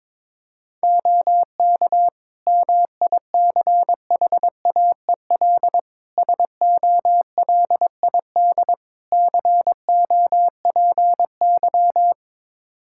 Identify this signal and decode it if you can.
Morse code, heard as a keyed tone, about 22 wpm, 700 Hz. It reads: OK MICHAEL SOLID COPY
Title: Practice QSO - Exchange 3 - More Info - 22wpm